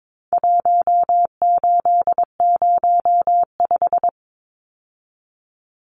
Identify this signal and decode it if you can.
Morse code, heard as a keyed tone, about 22 wpm, 700 Hz. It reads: 1805
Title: Numbers - Contest Style - 4-Digit 22wpm